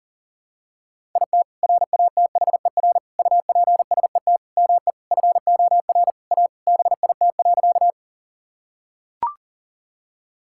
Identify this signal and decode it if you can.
Morse code, heard as a keyed tone, about 37 wpm, 700 Hz. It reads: IT RATHER UPSET ME FOR A BIT.
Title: Sentences from Top 1000 Words - Review Part 5 - 40wpm